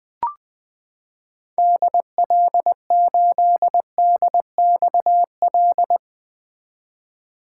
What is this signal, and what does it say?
Morse code, heard as a keyed tone, about 20 wpm, 700 Hz. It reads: DL8DXL
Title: 2023 SOTA Chaser Call Signs - 20wpm